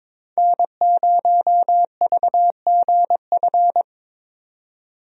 Morse code, heard as a keyed tone, about 22 wpm, 700 Hz. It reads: N0VGF
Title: USA Call Signs - 5 Characters - 22wpm